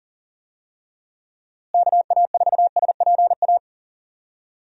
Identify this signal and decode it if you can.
Morse code, heard as a keyed tone, about 40 wpm, 700 Hz. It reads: KA4SPA
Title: USA Call Signs - 6 Characters - Repeated 2x - 40wpm